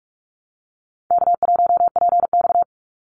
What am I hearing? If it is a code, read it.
Morse code, heard as a keyed tone, about 44 wpm, 700 Hz. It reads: K1PX
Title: Easy Call Signs Repeated 3x - 45wpm